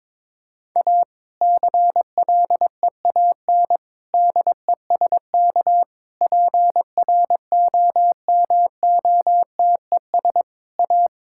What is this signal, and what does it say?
Morse code, heard as a keyed tone, about 21 wpm, 700 Hz. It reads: A CLEAN DESK PROMOTES A
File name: Difficult Sentences with Hints - 22wpm